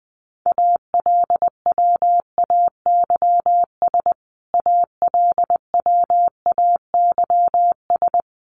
Morse code, heard as a keyed tone, about 20 wpm, 700 Hz. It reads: ALWAYS ALWAYS
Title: Top 500 Words Repeated 20wpm